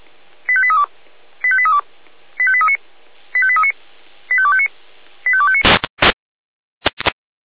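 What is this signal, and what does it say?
Five-tone selective call (ZVEI), heard as quick runs of five tones, tones of about 70 ms. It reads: 85832 85832 85839 85839 85259 85259
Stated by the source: ZVEI five-tone selective call received by SDR.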